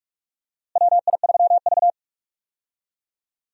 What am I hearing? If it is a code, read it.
Morse code, heard as a keyed tone, about 46 wpm, 700 Hz. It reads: WI3V
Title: USA Call Signs - 4 Characters - Repeated 2x - 45wpm